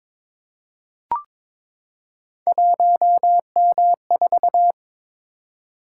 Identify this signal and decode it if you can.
Morse code, heard as a keyed tone, about 23 wpm, 700 Hz. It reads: 1M4
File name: practice / 22wpm / Three Letter-Number Combinations - 22wpm